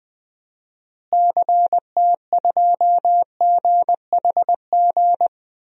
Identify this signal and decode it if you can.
Morse code, heard as a keyed tone, about 20 wpm, 700 Hz. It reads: CT2GHG
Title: DXCC-100 Call Signs - 6 Characters - Repeated 3x - 20wpm